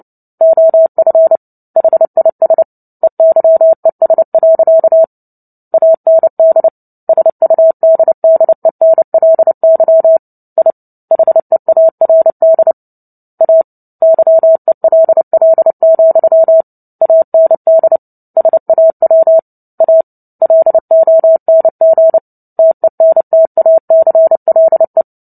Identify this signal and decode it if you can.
Morse code, heard as a keyed tone, about 30 wpm, 650 Hz. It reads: OF HIS EYES. AND SUDDENLY I HEARD A YELL, AND SAW A LONG TENTACLE
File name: Book Two / 03-Chapter 3 - 29 WPM